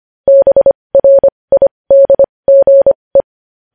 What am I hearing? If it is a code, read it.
Morse code, heard as a keyed tone, about 25 wpm, spaced as for 20 wpm, 550 Hz. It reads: BRIDGE
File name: podcast / News headlines-1x-25x20